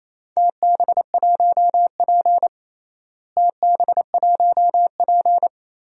Morse code, heard as a keyed tone, about 27 wpm, 700 Hz. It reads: TB1P TB1P
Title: DXCC-100 Call Signs - 4 Characters - Repeated 3x - 28wpm